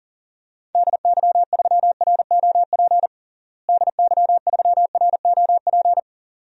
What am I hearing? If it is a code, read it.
Morse code, heard as a keyed tone, about 40 wpm, 700 Hz. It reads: DY3ROP DY3ROP
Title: DXCC-100 Call Signs - 6 Characters - Repeated 2x - 40wpm